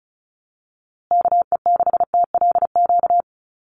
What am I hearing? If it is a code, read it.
Morse code, heard as a keyed tone, about 34 wpm, 700 Hz. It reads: KE6TLQ
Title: USA Call Signs Repeated 2x 35wpm